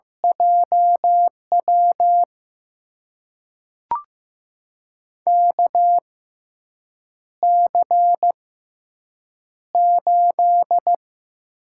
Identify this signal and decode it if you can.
Morse code, heard as a keyed tone, about 15 wpm, spaced as for 7 wpm, 700 Hz. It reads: JW KC8
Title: USA Call Signs - 6 Characters - 6x Character Spacing - 15wpm